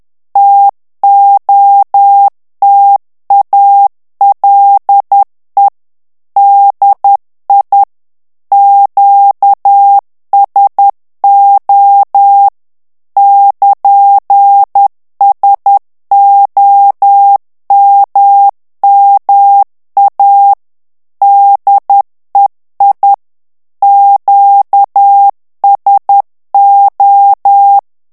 Morse code, heard as a keyed tone, about 10 wpm, 800 Hz. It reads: TOTALE DI QSO <KN>SOMMA DEI QSO
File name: ITALIEN texte clair 2 vitesse 10 mots minute